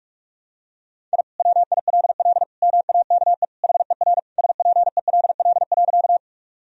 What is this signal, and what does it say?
Morse code, heard as a keyed tone, about 47 wpm, 700 Hz. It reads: I WILL MAKE HER SPELL.
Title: Sentences from Top 1000 Words - Review Part 1 - 45wpm